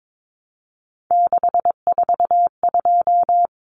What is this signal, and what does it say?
Morse code, heard as a keyed tone, about 22 wpm, 700 Hz. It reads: 642
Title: Numbers - 3-Digit 22wpm